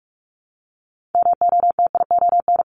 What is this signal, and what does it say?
Morse code, heard as a keyed tone, about 46 wpm, 700 Hz. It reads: MOTION
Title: Multi-letter Phonemes Repeated 3x 45wpm